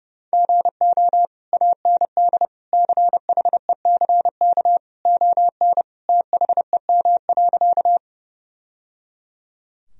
Morse code, heard as a keyed tone, about 30 wpm, 700 Hz. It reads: GO AND CHECK ON THEM.
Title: Sentences from Top 1000 Words - Review Part 3 - 30wpm